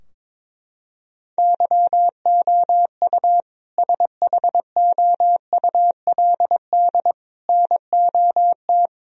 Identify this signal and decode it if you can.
Morse code, heard as a keyed tone, about 22 wpm, 700 Hz. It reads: YOU SHOULD NOT
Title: Sentences from Top 1000 Words - Review Part 3 - 22wpm